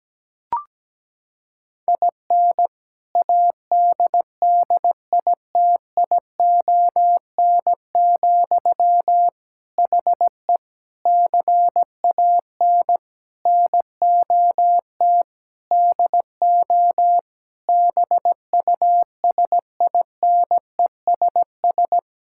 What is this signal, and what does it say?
Morse code, heard as a keyed tone, about 17 wpm, 700 Hz. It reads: IN ADDITION, HE CAN NOT DO BUSINESS
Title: Sentences from Top 1000 Words - Review Part 6 - 17wpm